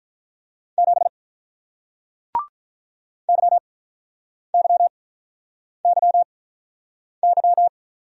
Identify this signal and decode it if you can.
Morse code, heard as a keyed tone, about 42 wpm, spaced as for 8 wpm, 700 Hz. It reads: 7 YYYY
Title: Speed-Racing Single Letter-Number 35wpm